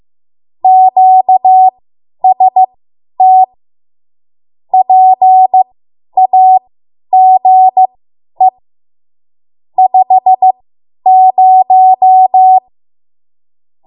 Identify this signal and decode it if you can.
Morse code, heard as a keyed tone, about 15 wpm, spaced as for 10 wpm, 750 Hz. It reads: QST PAGE 50
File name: Morse / 10WPM